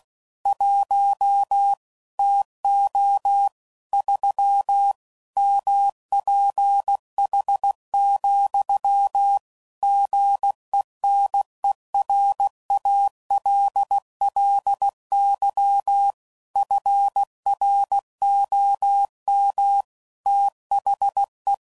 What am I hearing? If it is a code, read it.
Morse code, heard as a keyed tone, about 16 wpm, 800 Hz. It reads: 1 TO 3 MPH, GENERALLY FROM THE
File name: CW Code Groups @15WPM / Weather Sets / Weather 5 1